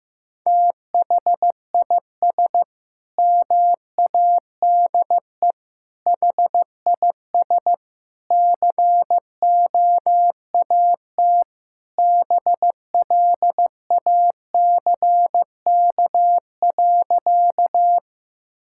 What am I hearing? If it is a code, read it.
Morse code, heard as a keyed tone, about 15 wpm, 700 Hz. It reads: THIS MADE HIS COAT BLACK.
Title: Sentences from Top 1600 Words 15wpm